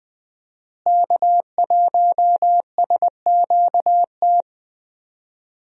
Morse code, heard as a keyed tone, about 20 wpm, 700 Hz. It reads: K1SQT
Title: USA Call Signs - 5 Characters - Repeated 3x - 20wpm